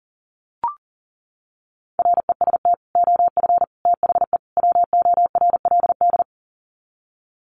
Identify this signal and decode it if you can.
Morse code, heard as a keyed tone, about 40 wpm, 700 Hz. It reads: REST OF THE WORLD
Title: Sets of 4 Words 40wpm